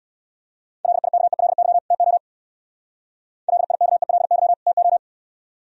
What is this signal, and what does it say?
Morse code, heard as a keyed tone, about 67 wpm, 700 Hz. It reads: LIKELY TO LIKELY TO